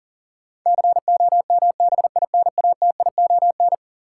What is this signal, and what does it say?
Morse code, heard as a keyed tone, about 40 wpm, 700 Hz. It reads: COMBINATION
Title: Sentences from Top 1000 Words - Review Part 6 - New Word Drill - 40wpm